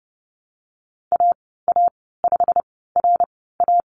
Morse code, heard as a keyed tone, about 30 wpm, spaced as for 17 wpm, 700 Hz. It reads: AA5RA
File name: Lesson 19g - Letter w - Call Signs 30wpm